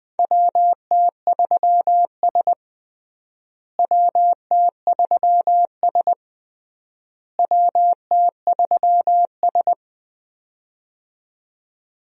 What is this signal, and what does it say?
Morse code, heard as a keyed tone, about 20 wpm, 700 Hz. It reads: WT3S WT3S WT3S
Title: USA Call Signs - 4 Characters - Repeated 3x - 20wpm